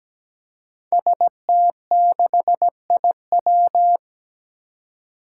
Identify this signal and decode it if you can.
Morse code, heard as a keyed tone, about 17 wpm, 700 Hz. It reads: ST6IW
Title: Five Challenging Letter-Number Combinations 17wpm